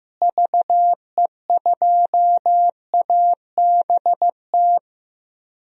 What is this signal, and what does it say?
Morse code, heard as a keyed tone, about 15 wpm, 700 Hz. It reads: VE2ABT
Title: DXCC-100 Call Signs - 6 Characters - Repeated 3x - 15wpm